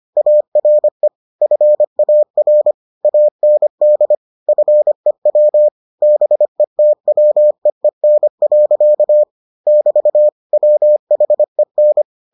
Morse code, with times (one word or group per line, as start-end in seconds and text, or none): 0.2-1.1 ARE
1.4-2.7 FAR
3.0-4.2 AND
4.5-5.7 FEW
6.0-9.2 BETWEEN.
9.7-10.3 <BT>
10.5-12.0 WHEN